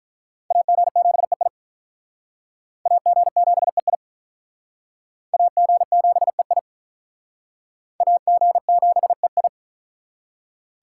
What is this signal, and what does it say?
Morse code, heard as a keyed tone, about 43 wpm, 700 Hz. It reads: AG7EI AG7EI AG7EI AG7EI
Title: Speed Racing - USA 2x2 Call Signs 35wpm